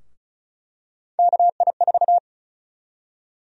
Morse code, heard as a keyed tone, about 34 wpm, 700 Hz. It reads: KI4
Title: USA Call Sign Prefixes - 35wpm